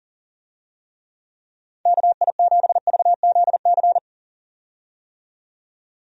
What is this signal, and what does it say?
Morse code, heard as a keyed tone, about 40 wpm, 700 Hz. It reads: KI7VZC
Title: USA Call Signs - 6 Characters - 40wpm